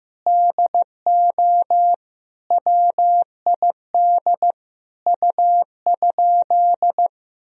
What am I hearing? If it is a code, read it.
Morse code, heard as a keyed tone, about 15 wpm, 700 Hz. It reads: DO WID U?
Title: Sentences from Top 100 Words and Abbreviations 15wpm